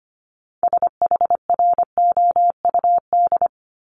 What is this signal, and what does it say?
Morse code, heard as a keyed tone, about 25 wpm, 700 Hz. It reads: SHROUD
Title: Multi-letter Phonemes 25wpm